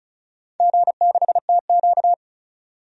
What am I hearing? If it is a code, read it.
Morse code, heard as a keyed tone, about 34 wpm, 700 Hz. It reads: G6TQ
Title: DXCC-100 Call Signs - 4 Characters - 35wpm